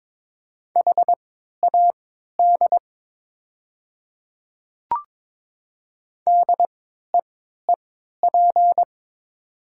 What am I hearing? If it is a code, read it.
Morse code, heard as a keyed tone, about 22 wpm, spaced as for 12 wpm, 700 Hz. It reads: HAD DEEP
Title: 3x Character Spacing - Top 500 Words - 22wpm